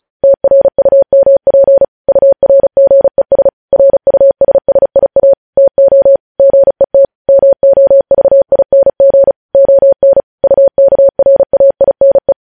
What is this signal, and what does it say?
Morse code, heard as a keyed tone, about 36 wpm, 550 Hz. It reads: TRUMP URGES RUSSIA TO GET MOVING ON UKRAINE
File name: podcast / News headlines-1x-35x35